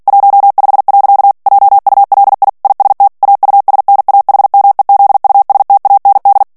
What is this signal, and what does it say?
Morse code, heard as a keyed tone, about 48 wpm, 800 Hz. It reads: 15. JULI IST AUSNAHMEZUSTAND
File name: morse mp3 / ALLEMAND texte clair 2 vitesse 45 mots minute